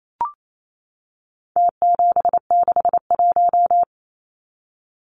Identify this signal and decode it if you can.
Morse code, heard as a keyed tone, about 28 wpm, 700 Hz. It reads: T761
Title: Numbers - Contest Style - 4-Digit 28wpm